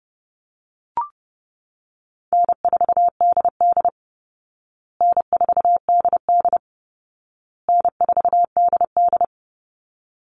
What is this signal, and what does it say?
Morse code, heard as a keyed tone, about 30 wpm, 700 Hz. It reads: N4DD N4DD N4DD
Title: Call Signs Repeated 2x - 30wpm